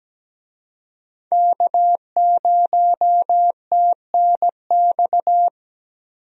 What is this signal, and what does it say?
Morse code, heard as a keyed tone, about 17 wpm, 700 Hz. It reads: K0TNX
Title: USA Call Signs - 5 Characters - Repeated 3x - 17wpm